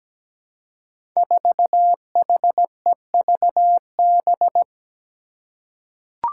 Morse code, heard as a keyed tone, about 17 wpm, 700 Hz. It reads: 4HEVB
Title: Five Challenging Letter-Number Combinations 17wpm